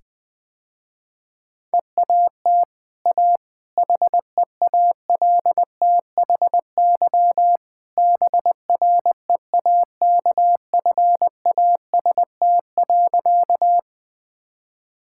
Morse code, heard as a keyed tone, about 20 wpm, 700 Hz. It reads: EAT A HEALTHY BREAKFAST.
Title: Easy Sentences with Hints - 20wpm